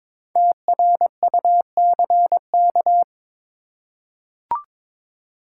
Morse code, heard as a keyed tone, about 22 wpm, 700 Hz. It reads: TRUCK
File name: 5 Letter Words - First 3 Letters Given - 22wpm